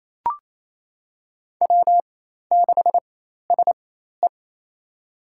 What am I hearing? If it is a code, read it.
Morse code, heard as a keyed tone, about 28 wpm, spaced as for 13 wpm, 700 Hz. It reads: W6SE
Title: USA Call Signs - 4 Characters - 4x Character Spacing - 28wpm